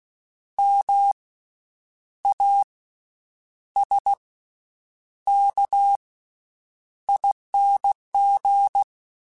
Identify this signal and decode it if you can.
Morse code, heard as a keyed tone, about 16 wpm, spaced as for 8 wpm, 800 Hz. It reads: MASKING
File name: Special INGs / Special INGs-mp3s / VerySlowToFaster Ing @15